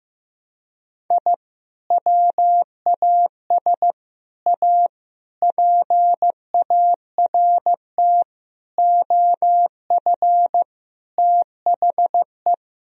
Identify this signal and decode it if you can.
Morse code, heard as a keyed tone, about 15 wpm, 700 Hz. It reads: I WAS A PART OF THE
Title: Sentences from Top 1000 Words - Review Part 1 - 15wpm